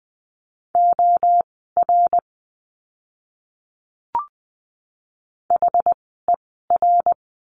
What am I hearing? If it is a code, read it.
Morse code, heard as a keyed tone, about 20 wpm, spaced as for 14 wpm, 700 Hz. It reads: OR HER
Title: 206 Lesson 33f - Letter j - Words 20wpm